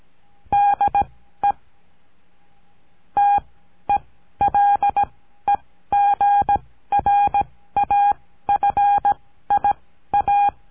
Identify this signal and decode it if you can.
Morse code, heard as a keyed tone, about 17 wpm, spaced as for 13 wpm, 810 Hz. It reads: DE TELEGRAFIA
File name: CURSO DE TELEGRAFIA